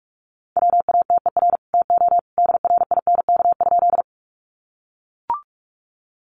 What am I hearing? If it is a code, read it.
Morse code, heard as a keyed tone, about 44 wpm, 700 Hz. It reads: WATER TO DRINK?
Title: Sentences from Top 1500 Words 45wpm